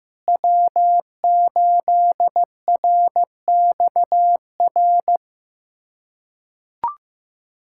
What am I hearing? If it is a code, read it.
Morse code, heard as a keyed tone, about 16 wpm, 700 Hz. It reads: W8RXR
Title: USA Callsign DE Callsign - 5 Characters - 15wpm